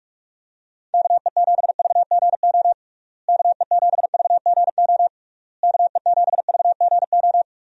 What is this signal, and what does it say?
Morse code, heard as a keyed tone, about 46 wpm, 700 Hz. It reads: KE7VGO KE7VGO KE7VGO
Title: USA Call Signs - 6 Characters - Repeated 3x - 45wpm